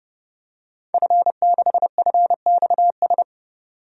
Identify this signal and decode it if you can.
Morse code, heard as a keyed tone, about 30 wpm, 700 Hz. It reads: F6FXS
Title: DXCC-100 Call Signs - 5 Characters - 30wpm